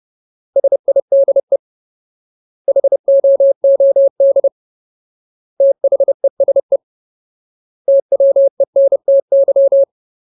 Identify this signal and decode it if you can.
Morse code, heard as a keyed tone, about 30 wpm, 550 Hz. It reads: SIDE HOOD THESE TWENTY
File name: Ultimate Word List / 30WPM Ultimate